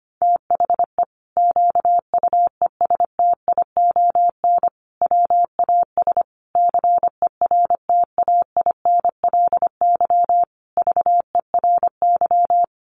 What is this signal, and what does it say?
Morse code, heard as a keyed tone, about 25 wpm, 700 Hz. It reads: THE QUESTION WAS CERTAINLY VERY
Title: Sentences from Top 1200 Words 25wpm